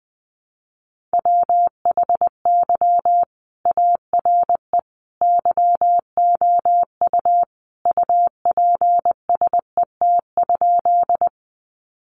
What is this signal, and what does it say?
Morse code, heard as a keyed tone, about 20 wpm, 700 Hz. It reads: WHY ARE YOU UPSET?
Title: Sentences from Top 800 Words 20wpm